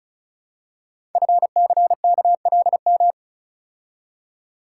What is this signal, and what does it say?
Morse code, heard as a keyed tone, about 36 wpm, 700 Hz. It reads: FCKLM
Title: Code Groups - 5 Letters - 35wpm